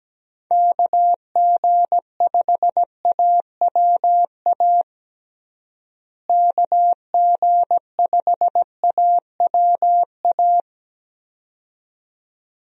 Morse code, heard as a keyed tone, about 17 wpm, 700 Hz. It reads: KG5AWA KG5AWA
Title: USA Call Signs - 6 Characters - Repeated 2x - 17wpm